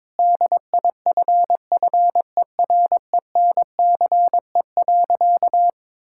Morse code, heard as a keyed tone, about 22 wpm, 700 Hz. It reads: DIFFERENCE.
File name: Difficult Sentences with Hints - 22wpm